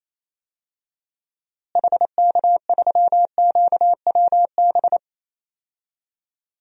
Morse code, heard as a keyed tone, about 27 wpm, 700 Hz. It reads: HK3QWB
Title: DXCC-100 Call Signs - 6 Characters - Repeated 2x - 28wpm